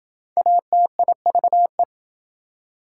Morse code, heard as a keyed tone, about 26 wpm, 700 Hz. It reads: ATIVE